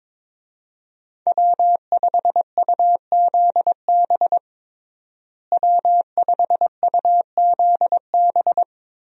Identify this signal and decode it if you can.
Morse code, heard as a keyed tone, about 22 wpm, 700 Hz. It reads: W5UZB W5UZB
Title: USA Call Signs - 5 Characters - Repeated 3x - 22wpm